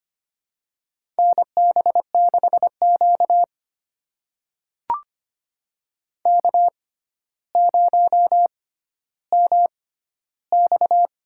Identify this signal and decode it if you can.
Morse code, heard as a keyed tone, about 25 wpm, spaced as for 13 wpm, 700 Hz. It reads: NB6Q K0MX
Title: USA Call Signs - 4 Characters - 6x Character Spacing - 25wpm